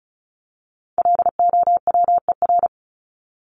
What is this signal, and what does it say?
Morse code, heard as a keyed tone, about 34 wpm, 700 Hz. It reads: LOWER
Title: Top 501-750 Words 35wpm